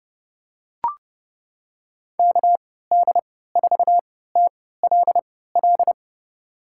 Morse code, heard as a keyed tone, about 30 wpm, spaced as for 17 wpm, 700 Hz. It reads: KD4TLL
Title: USA Call Signs - 3x Character Spacing - 30wpm